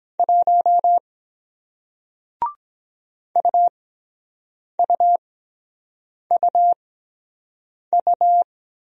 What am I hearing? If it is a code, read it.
Morse code, heard as a keyed tone, about 23 wpm, spaced as for 7 wpm, 700 Hz. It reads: 1 UUUU